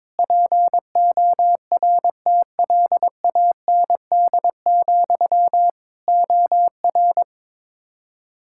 Morse code, heard as a keyed tone, about 22 wpm, 700 Hz. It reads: PORTLAND, OR
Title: Practice QSO - Exchange 1 - Beginning - 22wpm